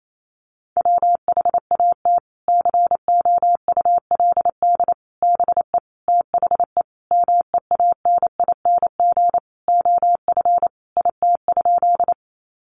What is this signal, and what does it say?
Morse code, heard as a keyed tone, about 28 wpm, 700 Hz. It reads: WHAT COULD BE THE MEANING OF IT?
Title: Sentences from Top 900 Words 28wpm